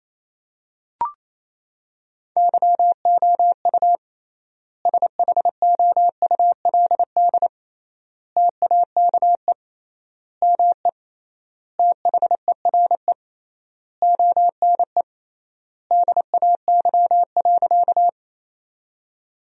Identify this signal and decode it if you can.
Morse code, heard as a keyed tone, about 28 wpm, 700 Hz. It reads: YOU SHOULD TAKE ME THERE ONE DAY.
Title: 3x Word-Spacing - Sentences from Top 300 Words - 28wpm